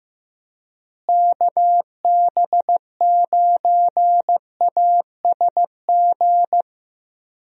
Morse code, heard as a keyed tone, about 15 wpm, 700 Hz. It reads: KB9ASG
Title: USA Call Signs - 6 Characters - Repeated 3x - 15wpm